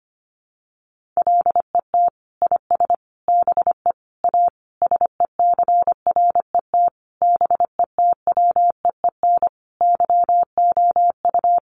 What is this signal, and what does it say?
Morse code, heard as a keyed tone, about 25 wpm, 700 Hz. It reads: LET IS BE A SECRET BETWEEN YOU
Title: Sentences from Top 1200 Words 25wpm